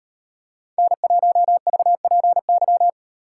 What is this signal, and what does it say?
Morse code, heard as a keyed tone, about 38 wpm, 700 Hz. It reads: N1VPY